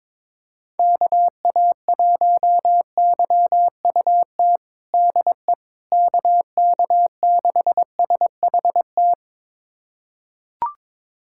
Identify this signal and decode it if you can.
Morse code, heard as a keyed tone, about 21 wpm, 700 Hz. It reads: KA1YUT DE KK6SHT
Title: USA Callsign DE Callsign 22wpm